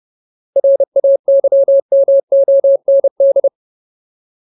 Morse code, heard as a keyed tone, about 30 wpm, 550 Hz. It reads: RAYMOND
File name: Ultimate Word List / 30WPM Ultimate